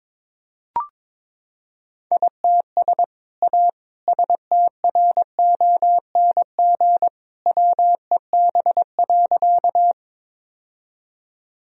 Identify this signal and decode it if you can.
Morse code, heard as a keyed tone, about 23 wpm, 700 Hz. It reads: ITS A STRONG WEB.
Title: Sentences from Top 1000 Words - Review Part 4 - 22wpm